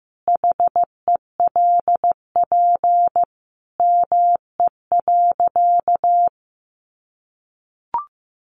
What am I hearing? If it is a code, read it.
Morse code, heard as a keyed tone, about 15 wpm, 700 Hz. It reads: HELP ME.
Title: Sentences from Top 200 Words 15wpm